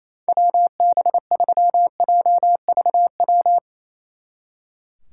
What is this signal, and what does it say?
Morse code, heard as a keyed tone, about 35 wpm, 700 Hz. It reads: WB3JVW